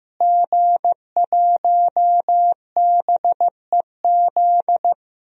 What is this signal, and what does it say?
Morse code, heard as a keyed tone, about 15 wpm, 700 Hz. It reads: G1BEZ
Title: DXCC-100 Call Signs - 5 Characters - Repeated 2x - 15wpm